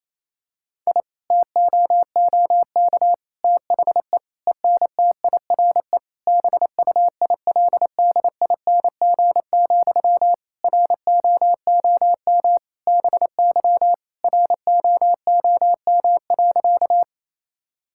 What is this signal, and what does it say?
Morse code, heard as a keyed tone, about 27 wpm, 700 Hz. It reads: I TOOK THE ENTIRE BUILDING, ROOM BY ROOM.
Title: Sentences from Top 1000 Words - Review Part 5 - 28wpm